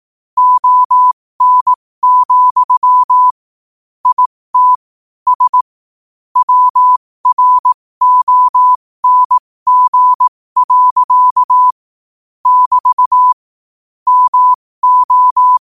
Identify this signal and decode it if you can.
Morse code, heard as a keyed tone, about 18 wpm, spaced as for 15 wpm, 1000 Hz. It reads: ON, IT S WRONG. <BT> MO
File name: qotdfiles / 15wpm